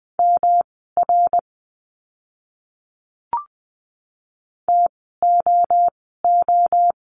Morse code, heard as a keyed tone, about 20 wpm, spaced as for 15 wpm, 700 Hz. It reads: MR TOO
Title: 244 Lesson 39f - Letter z - Words 20wpm